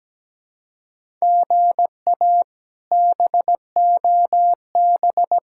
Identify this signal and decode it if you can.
Morse code, heard as a keyed tone, about 17 wpm, 700 Hz. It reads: GA BOB
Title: K1USN SST QSO Practice - 17wpm